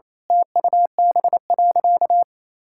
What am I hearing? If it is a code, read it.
Morse code, heard as a keyed tone, about 28 wpm, 700 Hz. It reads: TUB.
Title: Pangram Challenge v2 - 28wpm